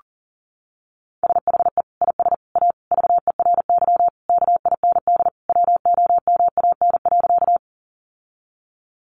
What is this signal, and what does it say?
Morse code, heard as a keyed tone, about 40 wpm, 700 Hz. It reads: SHE IS A VERY KIND WOMAN.
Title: Sentences from Top 1100 Words 40wpm